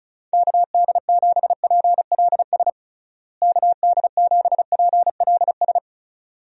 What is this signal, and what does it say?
Morse code, heard as a keyed tone, about 34 wpm, 700 Hz. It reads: KD7PLS KD7PLS
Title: USA Call Signs - 6 Characters - Repeated 2x - 35wpm